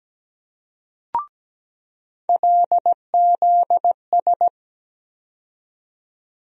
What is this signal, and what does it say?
Morse code, heard as a keyed tone, about 17 wpm, 700 Hz. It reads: LZS
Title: USA Call Sign Postfixes - 3 Letters 17wpm